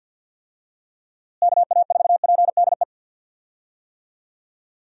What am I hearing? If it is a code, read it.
Morse code, heard as a keyed tone, about 51 wpm, 700 Hz. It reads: KA4PDE
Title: USA Call Signs - 6 Characters - Repeated 2x - 50wpm